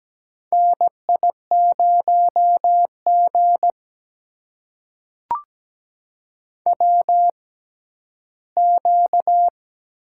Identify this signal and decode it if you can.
Morse code, heard as a keyed tone, about 17 wpm, spaced as for 12 wpm, 700 Hz. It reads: NI0G WQ
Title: USA Call Signs - 4 Characters - 6x Character Spacing - 17wpm